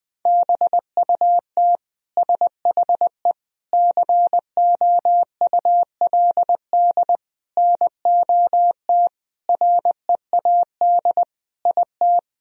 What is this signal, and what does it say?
Morse code, heard as a keyed tone, about 20 wpm, 700 Hz. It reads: BUT SHE COULD NOT READ IT
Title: Sentences from Top 1000 Words - Review Part 4 - 20wpm